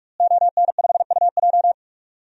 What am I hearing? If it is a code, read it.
Morse code, heard as a keyed tone, about 46 wpm, 700 Hz. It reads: ON5UJ
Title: DXCC-100 Call Signs - 5 Characters - Repeated 2x - 45wpm